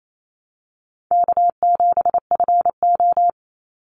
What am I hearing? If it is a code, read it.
Morse code, heard as a keyed tone, about 28 wpm, 700 Hz. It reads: K7FO
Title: NA SOTA Chaser Callsigns 28wpm